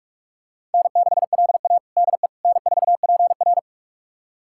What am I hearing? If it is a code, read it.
Morse code, heard as a keyed tone, about 42 wpm, 700 Hz. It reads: N6LA DE N4PR
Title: USA Callsign DE Callsign - 4 Characters - 45wpm